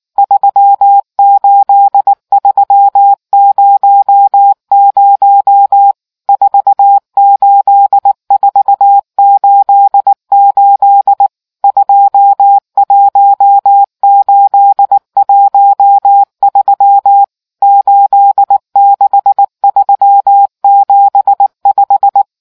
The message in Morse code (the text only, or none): 38300 48488 21813 86375